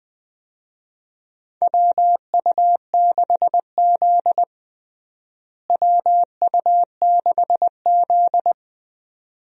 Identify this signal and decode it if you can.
Morse code, heard as a keyed tone, about 21 wpm, 700 Hz. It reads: WU6Z WU6Z
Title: USA Call Signs - 4 Characters - Repeated 2x - 20wpm